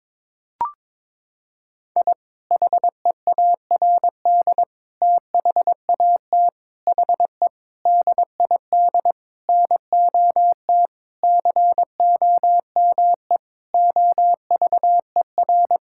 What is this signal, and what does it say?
Morse code, heard as a keyed tone, about 22 wpm, 700 Hz. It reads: I HEARD THAT HE DID NOT COME OVER
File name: practice / 22wpm / Sentences with 4-Word Phrases - 22wpm